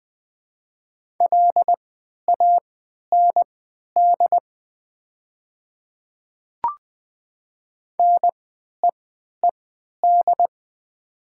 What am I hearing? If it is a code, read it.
Morse code, heard as a keyed tone, about 20 wpm, spaced as for 11 wpm, 700 Hz. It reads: LAND NEED
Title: 3x Character Spacing - Top 200 Words - 20wpm